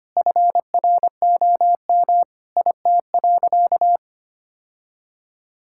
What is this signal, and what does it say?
Morse code, heard as a keyed tone, about 25 wpm, 700 Hz. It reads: FROM IT.
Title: Sentences from Top 1000 Words - Review Part 6 - 25wpm